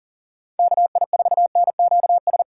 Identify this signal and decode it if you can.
Morse code, heard as a keyed tone, about 40 wpm, 700 Hz. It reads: KI4NQS
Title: USA Call Signs - 6 Characters - 4x Character Spacing - 40wpm